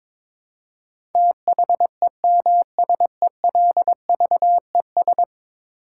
Morse code, heard as a keyed tone, about 22 wpm, 700 Hz. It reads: THEMSELVES
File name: practice / 22wpm / 10 Letter Words - First 3 Letters Given - 22wpm